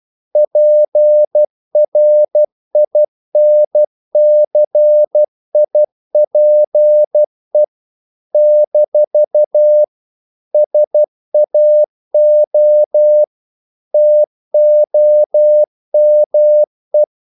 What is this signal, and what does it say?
Morse code, heard as a keyed tone, about 12 wpm, 600 Hz. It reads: PRINCIPE - SAO TOME
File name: CW / Practice / NationalCapitals12